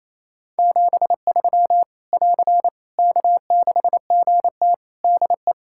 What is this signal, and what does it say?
Morse code, heard as a keyed tone, about 28 wpm, 700 Hz. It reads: 73 <AR> K6GT DE
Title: Practice QSO - Exchange 4 - Ending - 28wpm